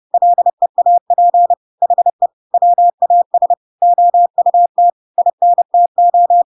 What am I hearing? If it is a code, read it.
Morse code, heard as a keyed tone, about 30 wpm, 700 Hz. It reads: LEAP HE WAS OUT INTO
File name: Pinocchio / Pinocchio ch3 30 30 ik1hge